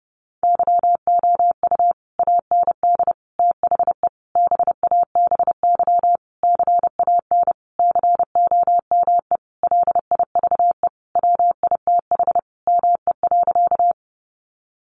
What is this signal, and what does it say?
Morse code, heard as a keyed tone, about 30 wpm, 700 Hz. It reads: YOU AND THE BABY CAN COME LIVE WITH ME.
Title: Sentences from Top 1300 Words 30wpm